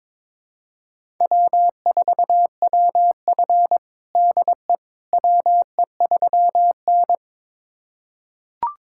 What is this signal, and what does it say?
Morse code, heard as a keyed tone, about 22 wpm, 700 Hz. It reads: W4WF DE WE3N
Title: USA Callsign DE Callsign - 4 Characters - 22wpm